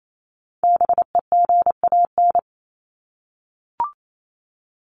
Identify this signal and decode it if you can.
Morse code, heard as a keyed tone, about 28 wpm, 700 Hz. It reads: BEGAN
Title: Top 300 Words 28wpm